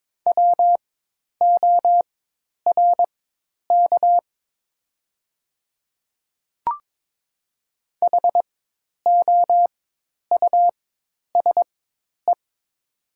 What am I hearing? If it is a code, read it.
Morse code, heard as a keyed tone, about 22 wpm, spaced as for 10 wpm, 700 Hz. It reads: WORK HOUSE
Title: 4x Character Spacing - Top 200 Words - 22wpm